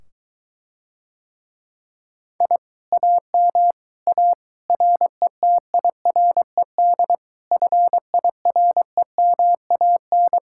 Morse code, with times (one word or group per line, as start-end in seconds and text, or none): 2.4-2.6 I
2.9-3.7 AM
4.1-4.3 A
4.7-7.1 RETIRED
7.5-10.4 FIREMAN